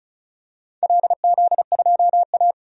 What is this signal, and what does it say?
Morse code, heard as a keyed tone, about 36 wpm, 700 Hz. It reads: LZ2A
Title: DXCC-100 Call Signs - 4 Characters - 35wpm